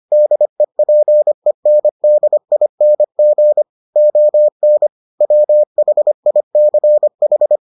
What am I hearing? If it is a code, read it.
Morse code, heard as a keyed tone, about 25 wpm, 600 Hz. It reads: DEPENDING ON WHICH